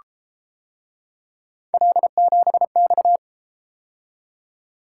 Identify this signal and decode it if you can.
Morse code, heard as a keyed tone, about 33 wpm, 700 Hz. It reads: L7X